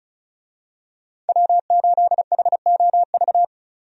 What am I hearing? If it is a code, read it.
Morse code, heard as a keyed tone, about 36 wpm, 700 Hz. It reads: W8HOV
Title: USA Call Signs - 5 Characters - 35wpm